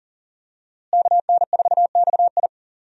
Morse code, heard as a keyed tone, about 37 wpm, 700 Hz. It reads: KN4XI
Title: USA Call Signs - 5 Characters - Repeated 3x - 40wpm